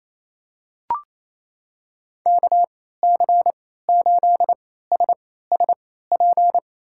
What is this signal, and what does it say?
Morse code, heard as a keyed tone, about 28 wpm, spaced as for 16 wpm, 700 Hz. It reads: KC8SSP
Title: USA Call Signs - 3x Character Spacing - 28wpm